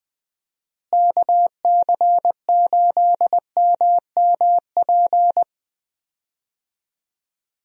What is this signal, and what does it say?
Morse code, heard as a keyed tone, about 20 wpm, 700 Hz. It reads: KC8MMP
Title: USA Call Signs - 6 Characters - Repeated 3x - 20wpm